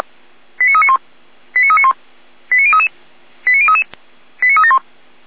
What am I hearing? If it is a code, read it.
Five-tone selective call (ZVEI), heard as quick runs of five tones, tones of about 70 ms. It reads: 89391 89391 89033 89033 89371